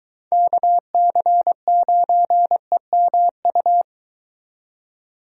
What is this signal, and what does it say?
Morse code, heard as a keyed tone, about 23 wpm, 700 Hz. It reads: KC9EMU